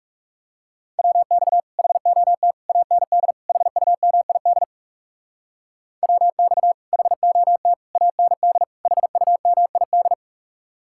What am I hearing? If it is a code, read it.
Morse code, heard as a keyed tone, about 40 wpm, 700 Hz. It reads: WX HOT AND HUMID WX HOT AND HUMID